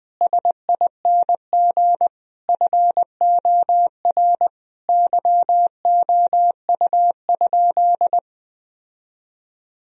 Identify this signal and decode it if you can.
Morse code, heard as a keyed tone, about 20 wpm, 700 Hz. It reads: SING FOR YOU?
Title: Sentences from Top 1000 Words - Review Part 2 - 20wpm